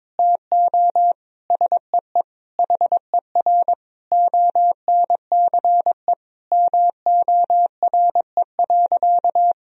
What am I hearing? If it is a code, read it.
Morse code, heard as a keyed tone, about 22 wpm, 700 Hz. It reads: TO SEE HER ONCE MORE.
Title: Sentences from Top 500 Words and Abbreviations 22wpm